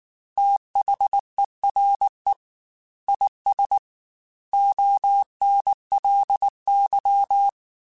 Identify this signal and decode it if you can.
Morse code, heard as a keyed tone, about 19 wpm, 800 Hz. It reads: THERE IS ONLY
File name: Technical Subjects / Technical Subjects-mp3s / AND Gate @18